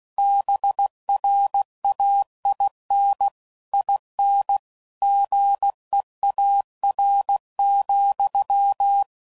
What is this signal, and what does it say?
Morse code, heard as a keyed tone, about 16 wpm, 800 Hz. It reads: BRAIN IN GEAR,